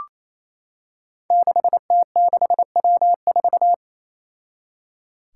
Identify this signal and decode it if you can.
Morse code, heard as a keyed tone, about 28 wpm, 700 Hz. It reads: 6T6W4
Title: Five Challenging Letter-Number Combinations 28wpm